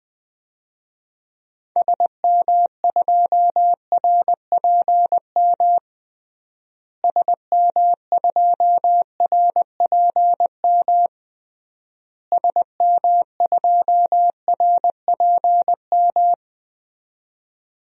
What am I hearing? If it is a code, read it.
Morse code, heard as a keyed tone, about 20 wpm, 700 Hz. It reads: SM2RPM SM2RPM SM2RPM
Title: DXCC-100 Call Signs - 6 Characters - Repeated 3x - 20wpm